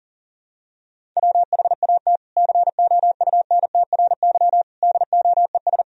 Morse code, heard as a keyed tone, about 37 wpm, 700 Hz. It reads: WHAT COUNTRY DOES
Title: Sentences from Top 300 Words and Abbreviations 40wpm